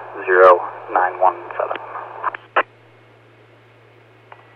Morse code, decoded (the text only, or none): none